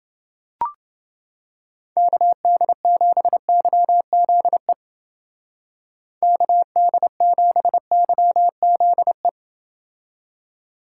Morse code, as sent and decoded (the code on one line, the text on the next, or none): -.- -.. --... -.-- --.. / -.- -.. --... -.-- --.. .
KD7YZE KD7YZE